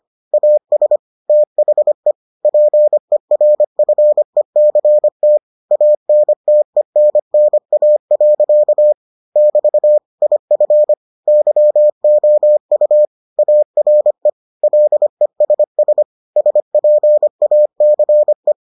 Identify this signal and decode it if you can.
Morse code, heard as a keyed tone, about 25 wpm, 600 Hz. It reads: AS THE PERFECT ANTENNA. <BT> IF YOU ARE LESS SPACE